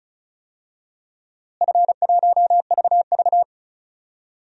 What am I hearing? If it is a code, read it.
Morse code, heard as a keyed tone, about 34 wpm, 700 Hz. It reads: F1VV
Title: DXCC-100 Call Signs - 4 Characters - Repeated 3x - 35wpm